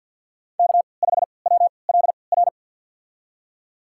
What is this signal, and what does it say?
Morse code, heard as a keyed tone, about 51 wpm, spaced as for 28 wpm, 700 Hz. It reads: K5WLR
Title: USA Call Signs - 5 Characters - 3x Character Spacing - 50wpm